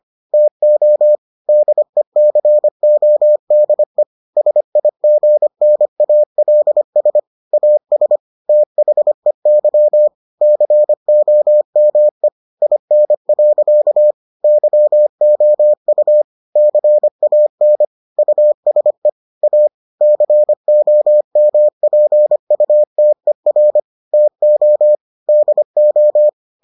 Morse code, heard as a keyed tone, about 25 wpm, 600 Hz. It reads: TO DECODE SIGNALS AS THEY COME IN. YOU CAN USE A COMPUTER TO DO